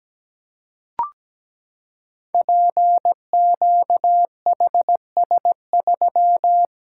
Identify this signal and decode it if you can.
Morse code, heard as a keyed tone, about 17 wpm, 700 Hz. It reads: PQHS3
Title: Code Groups - 5 Letter:Numbers - 17wpm